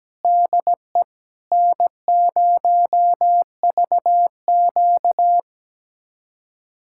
Morse code, heard as a keyed tone, about 17 wpm, 700 Hz. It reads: DE N0VQ
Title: USA Callsign DE Callsign - 4 Characters - 17wpm